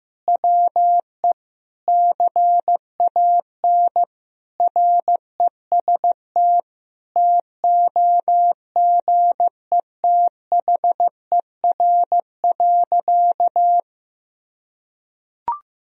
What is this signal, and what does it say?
Morse code, heard as a keyed tone, about 16 wpm, 700 Hz. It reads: WE CAN REST TOGETHER.
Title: Sentences from Top 500 Words and Abbreviations 15wpm